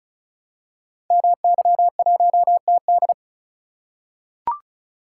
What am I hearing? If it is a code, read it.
Morse code, heard as a keyed tone, about 36 wpm, 690 Hz. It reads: MY1TD
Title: Code Groups - 5 Letter:Numbers - 35wpm